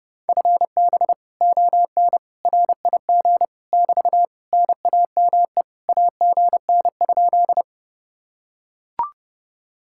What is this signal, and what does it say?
Morse code, heard as a keyed tone, about 30 wpm, 700 Hz. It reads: FB ON RIG <BT> NAME AGN?